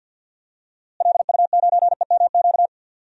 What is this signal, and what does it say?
Morse code, heard as a keyed tone, about 50 wpm, 700 Hz. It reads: LU9EMQ
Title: DXCC-100 Call Signs - 6 Characters - Repeated 2x - 50wpm